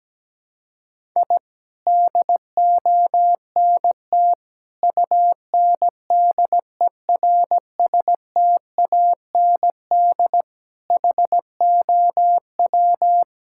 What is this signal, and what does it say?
Morse code, heard as a keyed tone, about 17 wpm, 700 Hz. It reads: I DONT UNDERSTAND HOW
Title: Sentences from Top 1000 Words - Review Part 5 - 17wpm